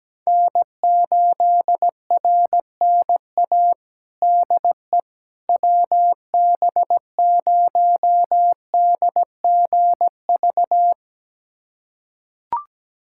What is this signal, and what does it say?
Morse code, heard as a keyed tone, about 17 wpm, 700 Hz. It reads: N8RNA DE WB0DGV
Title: USA Callsign DE Callsign 17wpm